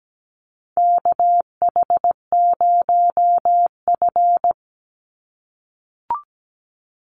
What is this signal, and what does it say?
Morse code, heard as a keyed tone, about 17 wpm, 700 Hz. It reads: KH0F
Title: USA Call Signs 17wpm